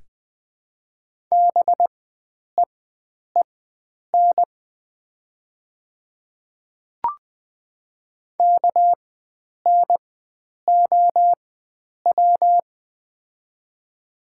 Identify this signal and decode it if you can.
Morse code, heard as a keyed tone, about 20 wpm, spaced as for 9 wpm, 700 Hz. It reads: BEEN KNOW
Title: 4x Character Spacing - Top 100 Words - 20wpm